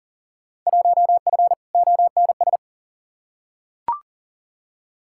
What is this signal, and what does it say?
Morse code, heard as a keyed tone, about 39 wpm, 700 Hz. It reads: 1F ONS
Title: Field Day Exchanges Encore - 40wpm